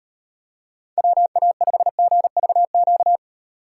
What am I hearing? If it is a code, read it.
Morse code, heard as a keyed tone, about 38 wpm, 700 Hz. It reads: WA5GVQ